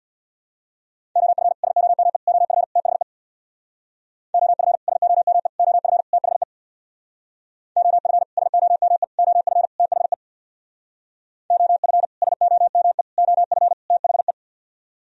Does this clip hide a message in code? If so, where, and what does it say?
Yes: Morse code from 1.2 to 14.3 s, heard as a keyed tone, about 63 wpm, 700 Hz. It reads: OF SOME OF THE OF SOME OF THE OF SOME OF THE OF SOME OF THE